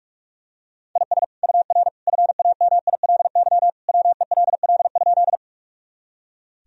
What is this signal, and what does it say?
Morse code, heard as a keyed tone, about 46 wpm, 700 Hz. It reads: IS UR FAMILY WELL?
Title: Sentences from Top 400 Words and Abbreviations 45wpm